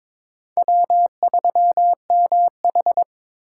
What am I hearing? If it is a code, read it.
Morse code, heard as a keyed tone, about 22 wpm, 700 Hz. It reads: W3MH
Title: USA Call Signs - 4 Characters - 2x Character Spacing - 22wpm